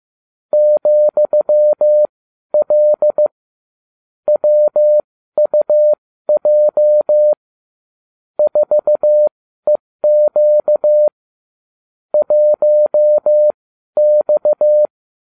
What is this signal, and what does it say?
Morse code, heard as a keyed tone, about 15 wpm, spaced as for 12 wpm, 600 Hz. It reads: ,L WUJ 4EQ 1X